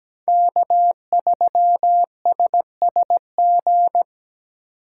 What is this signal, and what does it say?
Morse code, heard as a keyed tone, about 20 wpm, 700 Hz. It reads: K3SSG
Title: Speed Racing - USA 1x3 Call Signs 17wpm